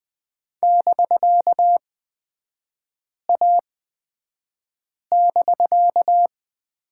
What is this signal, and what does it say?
Morse code, heard as a keyed tone, about 20 wpm, spaced as for 5 wpm, 700 Hz. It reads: <BK>A<BK>
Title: 250 Lesson 40e - BK - Three Characters 20wpm